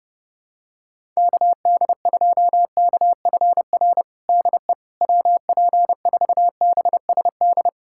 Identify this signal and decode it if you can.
Morse code, heard as a keyed tone, about 30 wpm, 700 Hz. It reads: KD2KFR DE WP4BSD
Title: USA Callsign DE Callsign 30wpm